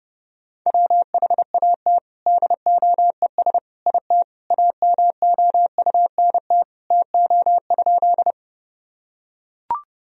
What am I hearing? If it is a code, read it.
Morse code, heard as a keyed tone, about 32 wpm, 700 Hz. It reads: WHAT DOES IT AMOUNT TO?
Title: Sentences from Top 1000 Words - Review Part 4 - 30wpm